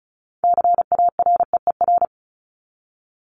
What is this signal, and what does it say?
Morse code, heard as a keyed tone, about 34 wpm, 700 Hz. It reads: CAREER
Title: Top 751-1000 Words 35wpm